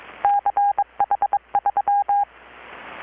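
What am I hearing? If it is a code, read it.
Morse code, heard as a keyed tone, about 23 wpm, 810 Hz. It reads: CH3
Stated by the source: Morse from KPH's Night of Nights XX maritime radiotelegraphy event, received in Chiapas, Mexico about 3,800 km away on a Kenwood R-600 with a grounded 10 m long-wire antenna.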